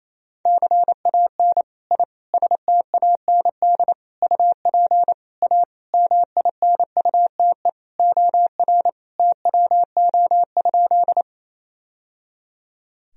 Morse code, heard as a keyed tone, about 28 wpm, 700 Hz. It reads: CAN I STAND UP A MINUTE OR TWO?
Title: Sentences from Top 1000 Words - Review Part 1 - 28wpm